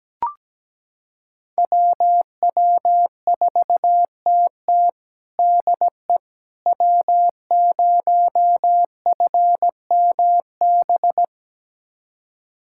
Morse code, heard as a keyed tone, about 17 wpm, 700 Hz. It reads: WW4TT DE W0FMB
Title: USA Callsign DE Callsign - 5 Characters - 17wpm